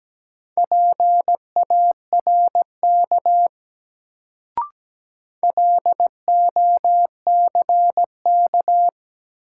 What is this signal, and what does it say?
Morse code, heard as a keyed tone, about 17 wpm, 700 Hz. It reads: PARK LOCK
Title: Sentences from Top 1300 Words - New Word Drill - 17wpm